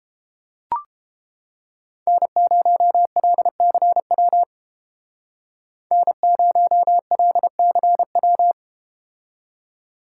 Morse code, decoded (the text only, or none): N0LCW N0LCW